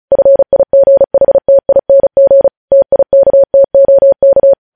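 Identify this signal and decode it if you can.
Morse code, heard as a keyed tone, about 36 wpm, 550 Hz. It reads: FIGHTING TIKTOK
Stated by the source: Morse code transmission of news headlines, preceded by vvv.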